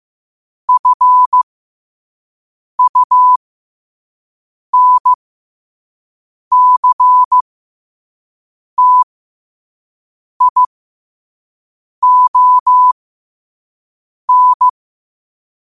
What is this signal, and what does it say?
Morse code, heard as a keyed tone, about 15 wpm, spaced as for 5 wpm, 1000 Hz. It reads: FUNCTION
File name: qotdfiles / 5wpm